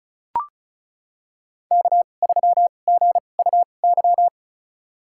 Morse code, heard as a keyed tone, about 34 wpm, spaced as for 25 wpm, 700 Hz. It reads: K3GUY
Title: USA Call Signs - 2x Character Spacing - 35wpm